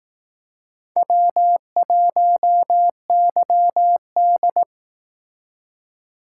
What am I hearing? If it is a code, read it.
Morse code, heard as a keyed tone, about 18 wpm, 700 Hz. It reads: W1YD
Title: Speed Racing - USA 1x2 Call Signs 15wpm